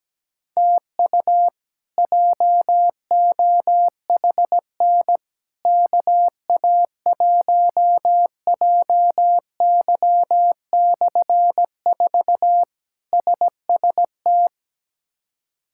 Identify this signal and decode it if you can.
Morse code, heard as a keyed tone, about 17 wpm, 700 Hz. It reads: TU JOHN KA1JY/4 SST
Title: K1USN SST QSO Practice - 17wpm